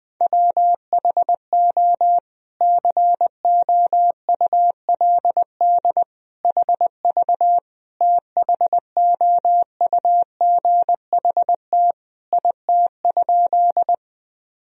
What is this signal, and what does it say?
Morse code, heard as a keyed tone, about 21 wpm, 700 Hz. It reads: WHO COULD HV THOUGHT IT?
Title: Sentences from Top 300 Words and Abbreviations 20wpm